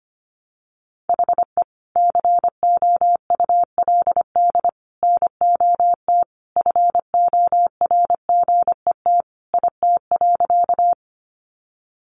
Morse code, heard as a keyed tone, about 25 wpm, 700 Hz. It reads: HE COULD NOT FORGET IT.
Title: Sentences from Top 1200 Words 25wpm